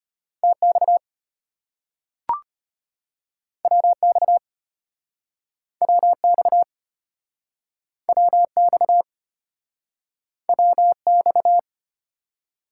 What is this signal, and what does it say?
Morse code, heard as a keyed tone, about 34 wpm, 700 Hz. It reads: TX WX WX WX WX